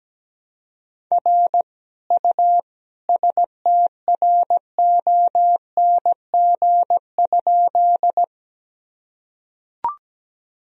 Morse code, heard as a keyed tone, about 17 wpm, 700 Hz. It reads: R U STRONG?
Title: Sentences from Top 500 Words and Abbreviations 17wpm